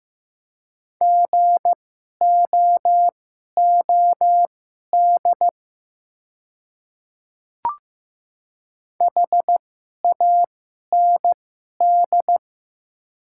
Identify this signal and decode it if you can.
Morse code, heard as a keyed tone, about 15 wpm, spaced as for 11 wpm, 700 Hz. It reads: GOOD HAND
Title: 2x Character Spacing - Top 200 Words - 15wpm